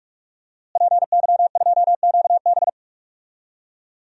Morse code, heard as a keyed tone, about 46 wpm, 700 Hz. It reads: PY2QB
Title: DXCC-100 Call Signs - 5 Characters - Repeated 2x - 45wpm